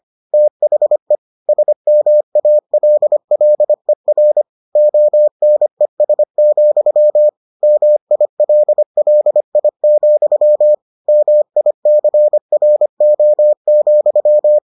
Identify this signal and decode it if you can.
Morse code, heard as a keyed tone, about 25 wpm, 600 Hz. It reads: THE SMALLER ONES, MILLI, MICRO,